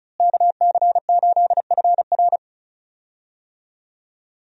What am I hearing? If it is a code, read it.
Morse code, heard as a keyed tone, about 36 wpm, 700 Hz. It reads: KC8FR
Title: USA Call Signs - 5 Characters - Repeated 2x - 35wpm